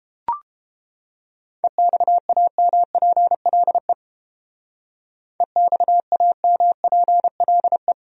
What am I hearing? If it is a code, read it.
Morse code, heard as a keyed tone, about 30 wpm, 700 Hz. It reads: EXAMPLE EXAMPLE